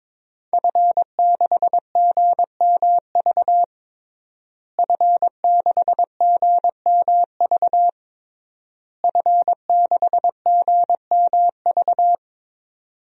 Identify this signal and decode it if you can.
Morse code, heard as a keyed tone, about 22 wpm, 700 Hz. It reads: F6GMV F6GMV F6GMV
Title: DXCC-100 Call Signs - 5 Characters - Repeated 3x - 22wpm